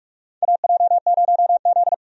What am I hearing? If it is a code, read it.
Morse code, heard as a keyed tone, about 41 wpm, 700 Hz. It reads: AJ0Z
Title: USA Call Signs - 4 Characters - 45wpm